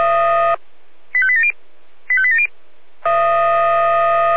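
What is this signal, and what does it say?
Five-tone selective call (ZVEI), heard as quick runs of five tones, tones of about 70 ms. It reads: 85870 85870
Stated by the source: ZVEI five-tone selective call 85875 received by SDR.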